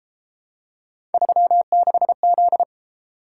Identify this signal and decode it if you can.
Morse code, heard as a keyed tone, about 32 wpm, 700 Hz. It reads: 36Z